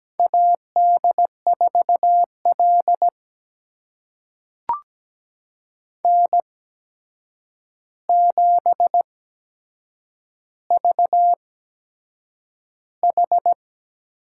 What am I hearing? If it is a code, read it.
Morse code, heard as a keyed tone, about 17 wpm, spaced as for 7 wpm, 700 Hz. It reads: AD4L N7VH
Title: USA Call Signs - 4 Characters - 8x Character Spacing - 17wpm